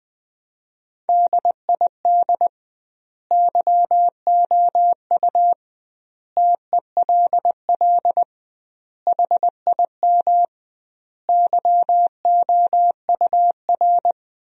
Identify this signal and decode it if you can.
Morse code, heard as a keyed tone, about 20 wpm, 700 Hz. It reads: DID YOU TELL HIM YOUR
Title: 2x Word-Spacing - Sentences from Top 400 Words - 20wpm